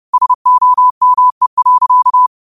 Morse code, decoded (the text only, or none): SOME.